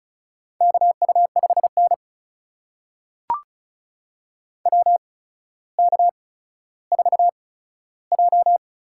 Morse code, heard as a keyed tone, about 36 wpm, spaced as for 15 wpm, 700 Hz. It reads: KU5N WK4J
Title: USA Call Signs - 4 Characters - 8x Character Spacing - 35wpm